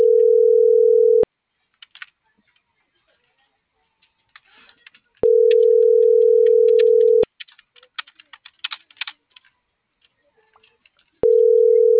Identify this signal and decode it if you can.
Ringback tone (North America), 440 + 480 Hz.
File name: CENTRAL TERRESTRE / out jessamynvillafuerte - .28970